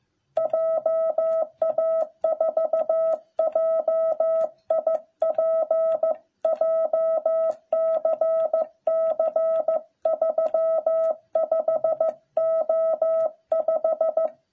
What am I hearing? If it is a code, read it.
Morse code, heard as a keyed tone, about 14 wpm, 660 Hz. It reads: JA4JIP JCC 35O5